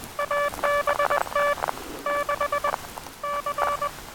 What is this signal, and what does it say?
Morse code, heard as a keyed tone, about 20 wpm, 1160 Hz, with noise. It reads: ABT 6 C
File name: attachments / 23.04.2017 DSB TRX OM3JA